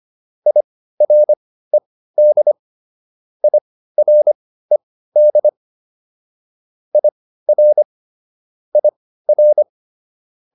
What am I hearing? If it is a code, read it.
Morse code, heard as a keyed tone, about 25 wpm, spaced as for 15 wpm, 600 Hz. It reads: IRED IRED IR IR